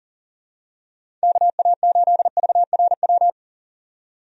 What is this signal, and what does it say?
Morse code, heard as a keyed tone, about 40 wpm, 690 Hz. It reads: KA8VRW
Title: USA Call Signs - 6 Characters - 40wpm